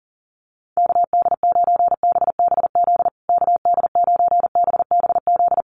Morse code, heard as a keyed tone, about 40 wpm, 700 Hz. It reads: KD9BBZ KD9BBZ
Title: USA Call Signs Repeated 3x 40wpm